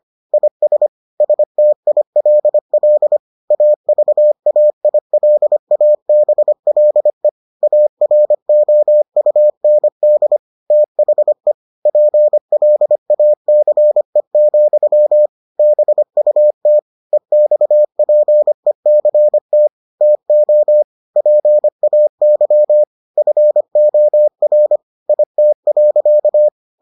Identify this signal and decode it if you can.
Morse code, heard as a keyed tone, about 25 wpm, 600 Hz. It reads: IS STILL AVAILABLE AROUND THE PLACE, BUT EXPECT TO PAY FOR IT.